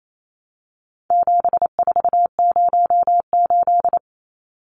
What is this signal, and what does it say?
Morse code, heard as a keyed tone, about 28 wpm, 700 Hz. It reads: 7408
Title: Numbers - 4-Digit 28wpm